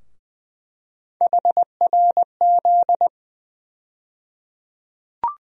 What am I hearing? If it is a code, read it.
Morse code, heard as a keyed tone, about 20 wpm, 700 Hz. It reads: HRZ
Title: USA Call Sign Postfixes - 3 Letters 20wpm